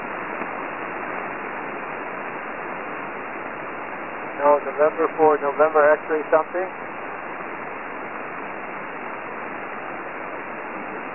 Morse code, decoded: none